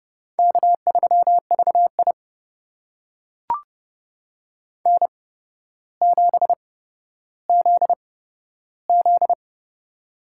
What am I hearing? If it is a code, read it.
Morse code, heard as a keyed tone, about 31 wpm, spaced as for 13 wpm, 700 Hz. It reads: K3VI N7ZZ
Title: USA Call Signs - 4 Characters - 8x Character Spacing - 30wpm